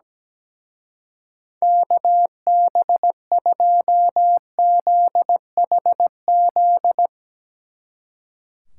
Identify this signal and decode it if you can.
Morse code, heard as a keyed tone, about 17 wpm, 700 Hz. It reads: KB2ZHZ
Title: Speed Racing - USA Call Signs 17wpm